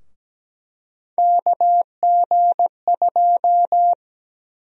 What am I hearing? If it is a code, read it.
Morse code, heard as a keyed tone, about 17 wpm, 700 Hz. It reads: KG2
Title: USA Call Sign Prefixes - 17wpm